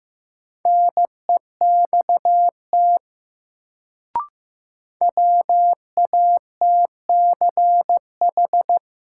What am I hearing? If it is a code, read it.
Morse code, heard as a keyed tone, about 15 wpm, 700 Hz. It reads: NEXT WATCH
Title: Sentences from Top 300 Words - New Word Drill - 15wpm